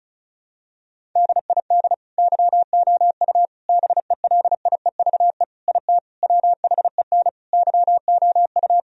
Morse code, heard as a keyed tone, about 36 wpm, 700 Hz. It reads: DID YOU BELIEVE IT WHEN YOU
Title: Sentences from Top 1000 Words - Review Part 1 - 35wpm